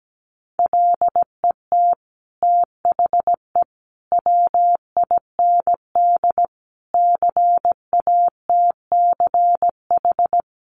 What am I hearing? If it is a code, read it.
Morse code, heard as a keyed tone, about 17 wpm, 700 Hz. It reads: LET THE WIND CATCH
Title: Sentences from Top 1500 Words 17wpm